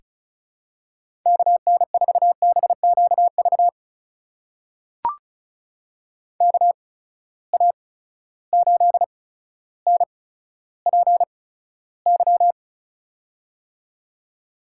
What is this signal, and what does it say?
Morse code, heard as a keyed tone, about 36 wpm, spaced as for 15 wpm, 700 Hz. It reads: KN4BQV KA8NPY
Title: USA Call Signs - 6 Characters - 8x Character Spacing - 35wpm